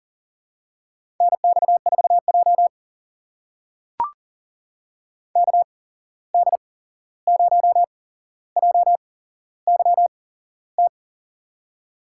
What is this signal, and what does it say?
Morse code, heard as a keyed tone, about 40 wpm, spaced as for 15 wpm, 700 Hz. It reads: NX4J KD0JYT
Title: USA Call Signs - 8x Character Spacing - 40wpm